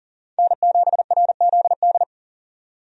Morse code, heard as a keyed tone, about 38 wpm, 700 Hz. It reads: N7RZD
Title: USA Call Signs - 5 Characters - 40wpm